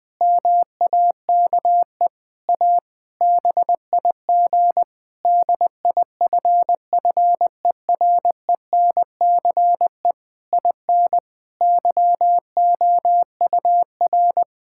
Morse code, heard as a keyed tone, about 21 wpm, 700 Hz. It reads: MAKE A BIG DIFFERENCE IN YOUR
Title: Sentences from Top 1000 Words - Review Part 5 - 20wpm